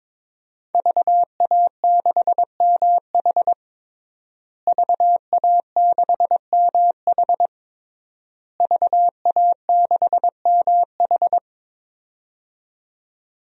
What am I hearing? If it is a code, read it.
Morse code, heard as a keyed tone, about 22 wpm, 700 Hz. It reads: VA6MH VA6MH VA6MH
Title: DXCC-100 Call Signs - 5 Characters - Repeated 3x - 22wpm